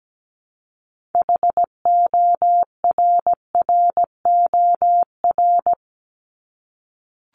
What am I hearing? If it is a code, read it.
Morse code, heard as a keyed tone, about 17 wpm, 700 Hz. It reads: HORROR
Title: Top 1501-1750 Words 17wpm